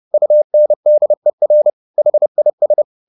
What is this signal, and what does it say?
Morse code, heard as a keyed tone, about 30 wpm, 600 Hz. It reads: UNDER HIS
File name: the old man and the sea / the old man and the sea 1 30wpm